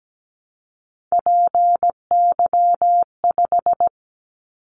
Morse code, heard as a keyed tone, about 17 wpm, 700 Hz. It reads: PY5
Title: Callsign Prefixes for Top 50 DXCC - 17wpm